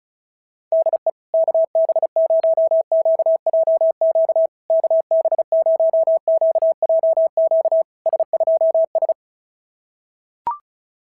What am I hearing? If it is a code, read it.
Morse code, heard as a keyed tone, about 36 wpm, 650 Hz. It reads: DE KB0QJQ KB0QJQ S2S
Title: SOTA QSO Practice 35wpm